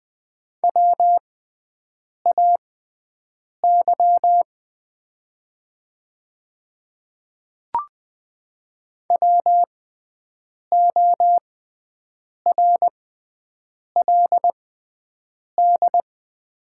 Morse code, heard as a keyed tone, about 20 wpm, spaced as for 7 wpm, 700 Hz. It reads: WAY WORLD
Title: 6x Character Spacing - Top 200 Words - 20wpm